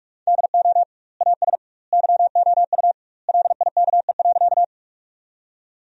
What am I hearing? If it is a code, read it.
Morse code, heard as a keyed tone, about 46 wpm, 700 Hz. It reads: DO AS YOU LIKE.
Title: 2x Word-Spacing - Sentences from Top 100 Words - 45wpm